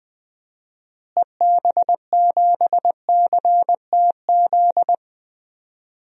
Morse code, heard as a keyed tone, about 21 wpm, 700 Hz. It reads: EB7CTZ
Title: DXCC-100 Call Signs - 6 Characters - Repeated 3x - 20wpm